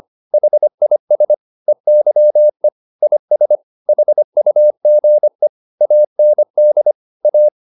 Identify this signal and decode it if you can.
Morse code, heard as a keyed tone, about 25 wpm, 600 Hz. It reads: HIS EYE IS HUGE AND A
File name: the old man and the sea part 1 25wpm